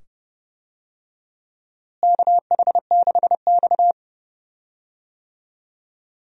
Morse code, heard as a keyed tone, about 29 wpm, 700 Hz. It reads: KH6X
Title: DXCC-100 Call Signs - 4 Characters - Repeated 2x - 30wpm